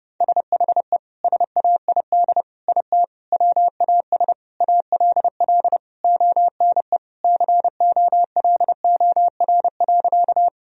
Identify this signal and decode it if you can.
Morse code, heard as a keyed tone, about 30 wpm, 700 Hz. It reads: SHE SAID IT WAS ALL ONE COLOR.
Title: Sentences from Top 1000 Words - Review Part 2 - 30wpm